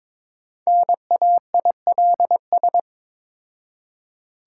Morse code, heard as a keyed tone, about 22 wpm, 690 Hz. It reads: NAILS
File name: Difficult Words with Hints - 22wpm